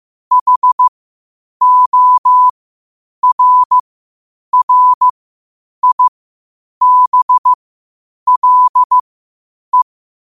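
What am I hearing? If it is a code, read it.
Morse code, heard as a keyed tone, about 15 wpm, spaced as for 8 wpm, 1000 Hz. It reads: HORRIBLE
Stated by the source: Morse code quotes at 8 WPM.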